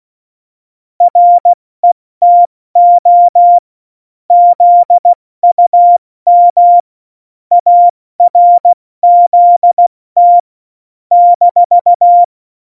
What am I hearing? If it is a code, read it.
Morse code, heard as a keyed tone, about 16 wpm, 700 Hz. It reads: RETO ZUM ARZT -
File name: Morsekurs / Witze / Witz 17 April 2018, 70 F80